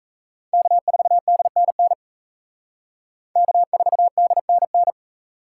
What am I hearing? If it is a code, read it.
Morse code, heard as a keyed tone, about 35 wpm, 700 Hz. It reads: K4DNN K4DNN